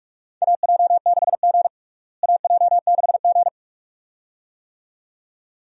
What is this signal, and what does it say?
Morse code, heard as a keyed tone, about 46 wpm, 700 Hz. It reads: AJ6G AJ6G
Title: USA Call Signs - 4 Characters - Repeated 2x - 45wpm